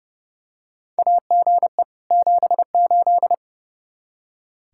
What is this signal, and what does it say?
Morse code, heard as a keyed tone, about 30 wpm, 700 Hz. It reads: AGE 78
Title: Practice QSO - Exchange 3 - More Info - 30wpm